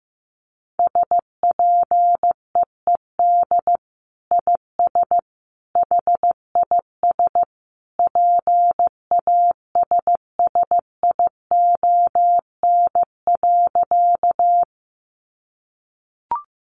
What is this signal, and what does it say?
Morse code, heard as a keyed tone, about 15 wpm, 700 Hz. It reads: SPEED IS HIS PASSION.
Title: Sentences from Top 1300 Words 15wpm